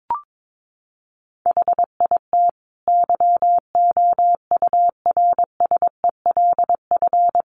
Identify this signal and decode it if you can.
Morse code, heard as a keyed tone, about 22 wpm, 700 Hz. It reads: HIT YOURSELF
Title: Sentences from Top 1200 Words 22wpm